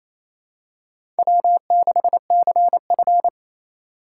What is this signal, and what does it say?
Morse code, heard as a keyed tone, about 28 wpm, 700 Hz. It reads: W6CF
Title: USA Call Signs - 4 Characters - 4x Character Spacing - 28wpm